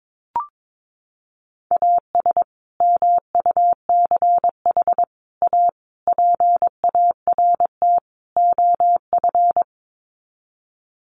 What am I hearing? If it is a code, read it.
Morse code, heard as a keyed tone, about 22 wpm, 700 Hz. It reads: AS MUCH A PART OF
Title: Sets of 5 Words Encore2 22wpm